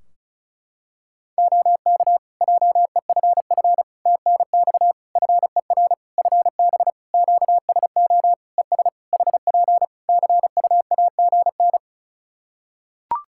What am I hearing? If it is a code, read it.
Morse code, heard as a keyed tone, about 34 wpm, 700 Hz. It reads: OK JEFF TNX FER FB QSO ES HP CUAGN
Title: Practice QSO - Exchange 4 - Ending - 35wpm